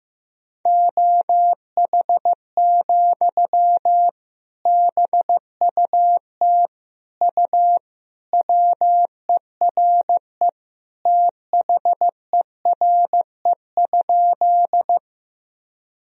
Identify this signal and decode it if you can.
Morse code, heard as a keyed tone, about 15 wpm, 700 Hz. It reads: OH, BUT U WERE THERE?
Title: Sentences from Top 500 Words and Abbreviations 15wpm